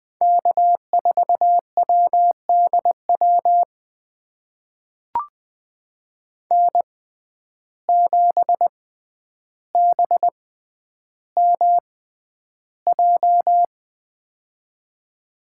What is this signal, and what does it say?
Morse code, heard as a keyed tone, about 20 wpm, spaced as for 10 wpm, 700 Hz. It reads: K4WDW N7BMJ
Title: USA Call Signs - 5 Characters - 6x Character Spacing - 20wpm